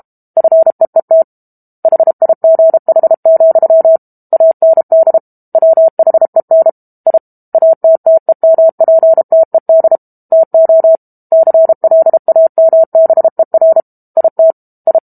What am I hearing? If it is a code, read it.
Morse code, heard as a keyed tone, about 32 wpm, 650 Hz. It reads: FEET HIGH, AND WHEN I ATTEMPTED TO CLAMBER IT I
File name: Book Two / 06-Chapter 6 - 32 WPM